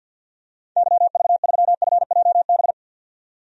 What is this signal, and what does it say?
Morse code, heard as a keyed tone, about 47 wpm, 700 Hz. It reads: YV3FJB
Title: DXCC-100 Call Signs - 6 Characters - 50wpm